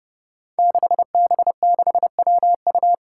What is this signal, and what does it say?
Morse code, heard as a keyed tone, about 30 wpm, 700 Hz. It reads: 6B6WU
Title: Five Challenging Letter-Number Combinations 30wpm